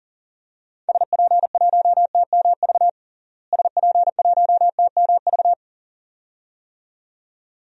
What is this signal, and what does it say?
Morse code, heard as a keyed tone, about 37 wpm, 700 Hz. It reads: SP1TMV SP1TMV
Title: DXCC-100 Call Signs - 6 Characters - Repeated 2x - 40wpm